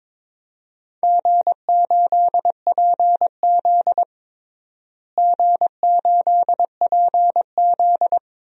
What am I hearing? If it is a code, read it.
Morse code, heard as a keyed tone, about 21 wpm, 700 Hz. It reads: G8PZ G8PZ
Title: DXCC-100 Call Signs - 4 Characters - Repeated 2x - 22wpm